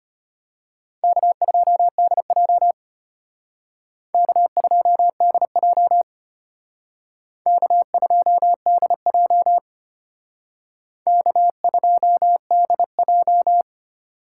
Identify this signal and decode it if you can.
Morse code, heard as a keyed tone, about 32 wpm, 700 Hz. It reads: K2DJ K2DJ K2DJ K2DJ